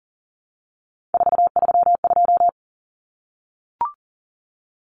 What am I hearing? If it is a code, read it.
Morse code, heard as a keyed tone, about 40 wpm, 700 Hz. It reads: 432
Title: Common Cut Numbers - 3-Digit 40wpm